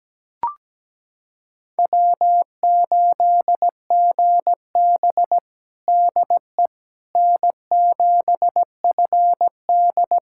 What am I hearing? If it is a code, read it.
Morse code, heard as a keyed tone, about 17 wpm, 700 Hz. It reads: W8GB DE N7FD
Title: USA Callsign DE Callsign - 4 Characters - 17wpm